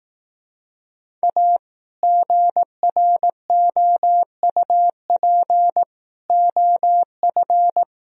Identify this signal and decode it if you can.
Morse code, heard as a keyed tone, about 18 wpm, 700 Hz. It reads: A GROUP OF
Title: Speed-Racing - Sets of 3 Words 15wpm